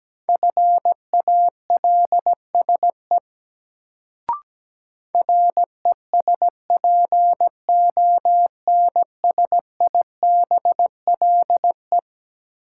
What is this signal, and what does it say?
Morse code, heard as a keyed tone, about 17 wpm, 700 Hz. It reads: FALSE RESPONSIBLE
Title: Sentences from Top 1600 Words - New Word Drill Alt - 17wpm